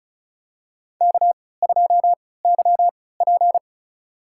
Morse code, heard as a keyed tone, about 36 wpm, spaced as for 20 wpm, 700 Hz. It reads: K2YP
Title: USA Call Signs - 3x Character Spacing - 35wpm